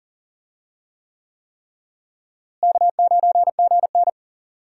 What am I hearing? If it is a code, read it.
Morse code, heard as a keyed tone, about 38 wpm, 700 Hz. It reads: K9GN
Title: USA Call Signs - 4 Characters - Repeated 3x - 40wpm